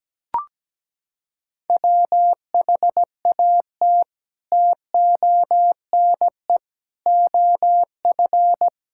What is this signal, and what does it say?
Morse code, heard as a keyed tone, about 17 wpm, 700 Hz. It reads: WHAT TONE OF
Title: Sentences from Top 1600 Words 17wpm